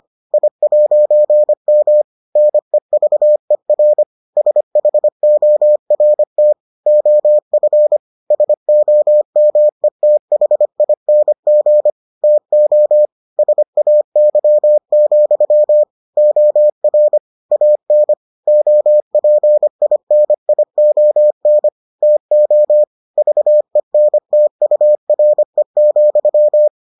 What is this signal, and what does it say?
Morse code, heard as a keyed tone, about 25 wpm, 600 Hz. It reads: I'M NEVER SHORT OF SOMETHING TO SAY, OR AN OPINION TO VENTURE,